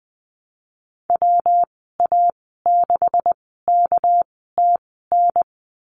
Morse code, heard as a keyed tone, about 20 wpm, spaced as for 15 wpm, 700 Hz. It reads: WA6KTN
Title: 200 Lesson 32g - Letter k - Call Signs 20wpm